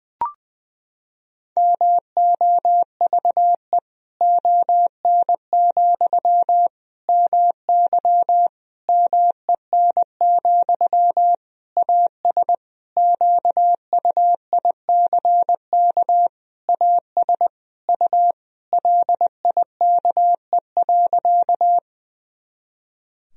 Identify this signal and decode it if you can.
Morse code, heard as a keyed tone, about 20 wpm, 700 Hz. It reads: MOVE ON, MY MEN, AS QUICK AS U LIKE.
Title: Sentences from Top 500 Words and Abbreviations 20wpm